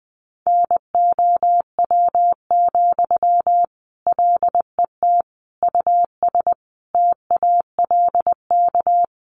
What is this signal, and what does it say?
Morse code, heard as a keyed tone, about 19 wpm, 700 Hz. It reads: NOW, LET US TALK
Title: Sentences from Top 800 Words 20wpm